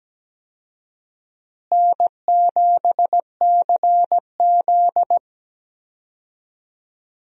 Morse code, heard as a keyed tone, about 17 wpm, 700 Hz. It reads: N7CZ
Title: USA Call Signs - 4 Characters - Repeated 3x - 17wpm